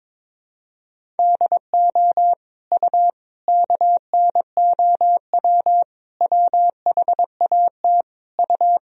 Morse code, heard as a keyed tone, about 22 wpm, 700 Hz. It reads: DO U KNOW WHAT U
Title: Sentences from Top 100 Words and Abbreviations 22wpm